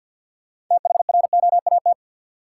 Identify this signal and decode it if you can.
Morse code, heard as a keyed tone, about 53 wpm, 700 Hz. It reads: THROAT
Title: Six Letter Words - 50wpm